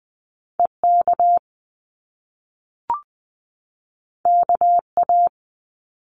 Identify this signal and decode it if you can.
Morse code, heard as a keyed tone, about 20 wpm, 700 Hz. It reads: EK KA
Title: Top Two Letter Combinations in Words 20wpm